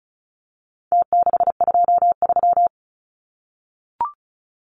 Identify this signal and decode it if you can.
Morse code, heard as a keyed tone, about 34 wpm, 700 Hz. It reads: T623
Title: Numbers - Contest Style - 4-Digit 35wpm